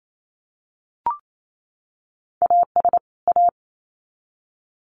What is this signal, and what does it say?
Morse code, heard as a keyed tone, about 28 wpm, 700 Hz. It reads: AS A
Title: Sets of 2 Words 28wpm